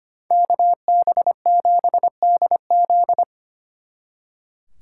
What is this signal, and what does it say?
Morse code, heard as a keyed tone, about 33 wpm, 700 Hz. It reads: KB7DZ